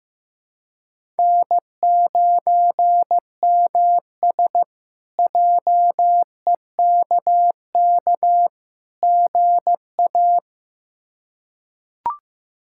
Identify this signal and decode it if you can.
Morse code, heard as a keyed tone, about 15 wpm, 700 Hz. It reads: N9MS JEKK GA
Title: K1USN SST QSO Practice - 15wpm